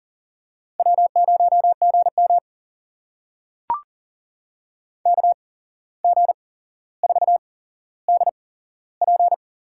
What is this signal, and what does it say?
Morse code, heard as a keyed tone, about 40 wpm, spaced as for 16 wpm, 700 Hz. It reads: W0GM KG4DP
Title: USA Call Signs - 8x Character Spacing - 40wpm